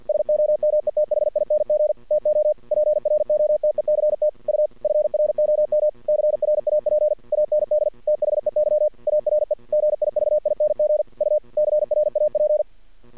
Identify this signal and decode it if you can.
Morse code, heard as a keyed tone, about 51 wpm, 600 Hz. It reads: SOMETHING TO PROTECT U FROM CRAP AND THEY ARE GIVING U CRAP
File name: archived / CRSnet